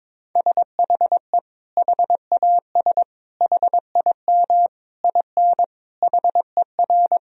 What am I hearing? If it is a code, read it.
Morse code, heard as a keyed tone, about 22 wpm, 700 Hz. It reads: SHE HAS HIM IN HER
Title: Sentences from Top 1000 Words - Review Part 4 - 22wpm